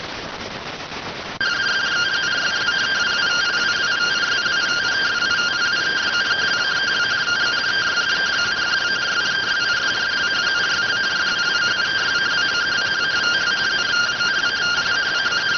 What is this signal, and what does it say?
Two-tone teleprinter signal, about 170 Hz shift.